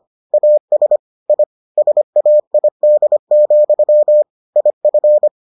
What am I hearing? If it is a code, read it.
Morse code, heard as a keyed tone, about 25 wpm, 600 Hz. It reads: AS I SAID, IF